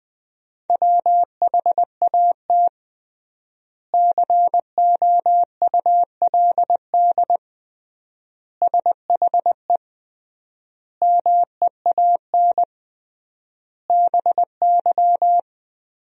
Morse code, heard as a keyed tone, about 20 wpm, 700 Hz. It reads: WHAT COULD SHE MEAN BY
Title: 3x Word-Spacing - Sentences from Top 200 Words - 20wpm